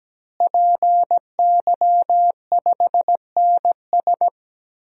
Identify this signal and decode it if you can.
Morse code, heard as a keyed tone, about 17 wpm, 700 Hz. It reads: PY5NS
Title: DXCC-100 Call Signs - 5 Characters - Repeated 3x - 17wpm